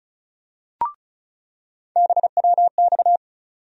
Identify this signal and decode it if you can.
Morse code, heard as a keyed tone, about 34 wpm, 700 Hz. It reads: BWX
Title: USA Call Sign Postfixes - 3 Letters 35wpm